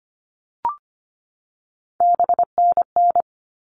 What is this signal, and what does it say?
Morse code, heard as a keyed tone, about 25 wpm, 700 Hz. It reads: BNN
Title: All Cut Numbers - 3-Digit 25wpm